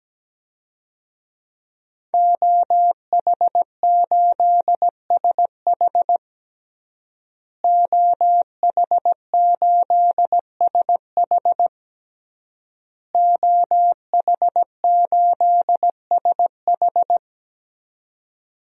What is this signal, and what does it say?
Morse code, heard as a keyed tone, about 17 wpm, 700 Hz. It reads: OH8SH OH8SH OH8SH
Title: DXCC-100 Call Signs - 5 Characters - Repeated 3x - 17wpm